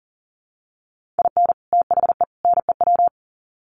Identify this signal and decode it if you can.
Morse code, heard as a keyed tone, about 40 wpm, 700 Hz. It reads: IN THE NEW
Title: Sets of 3 Words 40wpm